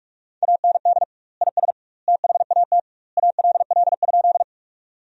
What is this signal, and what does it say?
Morse code, heard as a keyed tone, about 46 wpm, 700 Hz. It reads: AND IS THAT ALL?
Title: 2x Word-Spacing - Sentences from Top 100 Words - 45wpm